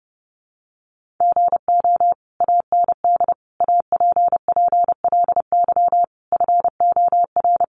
Morse code, heard as a keyed tone, about 30 wpm, 700 Hz. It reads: GO AND APPLY FOR
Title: Sentences from Top 1200 Words 30wpm